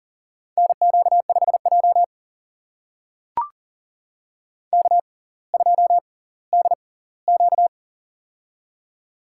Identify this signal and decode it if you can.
Morse code, heard as a keyed tone, about 40 wpm, spaced as for 20 wpm, 700 Hz. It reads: NQ5J K2DQ
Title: USA Call Signs - 4 Characters - 6x Character Spacing - 40wpm